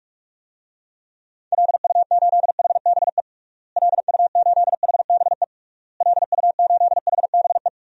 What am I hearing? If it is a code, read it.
Morse code, heard as a keyed tone, about 41 wpm, 700 Hz. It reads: LU8HBE LU8HBE LU8HBE
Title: DXCC-100 Call Signs - 6 Characters - Repeated 3x - 45wpm